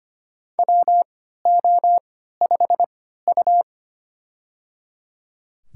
Morse code, heard as a keyed tone, about 25 wpm, spaced as for 14 wpm, 700 Hz. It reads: WO5U
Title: USA Call Signs - 4 Characters - 3x Character Spacing - 25wpm